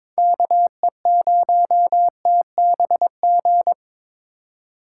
Morse code, heard as a keyed tone, about 22 wpm, 700 Hz. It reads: KE0TBG
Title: USA Call Signs - 6 Characters - 2x Character Spacing - 22wpm